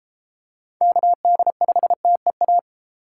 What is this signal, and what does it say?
Morse code, heard as a keyed tone, about 33 wpm, 700 Hz. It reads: KD5TEA